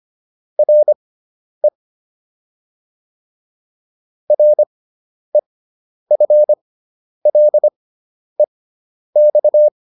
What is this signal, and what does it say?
Morse code, heard as a keyed tone, about 25 wpm, spaced as for 10 wpm, 600 Hz. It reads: RE REFLEX
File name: RE6 10